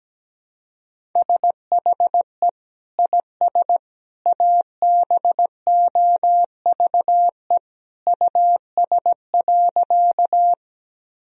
Morse code, heard as a keyed tone, about 17 wpm, 700 Hz. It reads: SHE IS ABOVE US.
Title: Sentences from Top 1000 Words - Review Part 2 - 17wpm